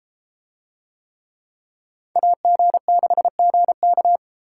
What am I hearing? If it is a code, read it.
Morse code, heard as a keyed tone, about 32 wpm, 700 Hz. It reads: AG6GK